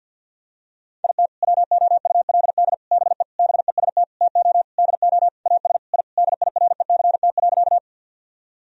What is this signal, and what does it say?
Morse code, heard as a keyed tone, about 46 wpm, 700 Hz. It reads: IT WOULD BE BEST TO DO AS I DIRECT.
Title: Sentences from Top 1000 Words - Review Part 2 - 50wpm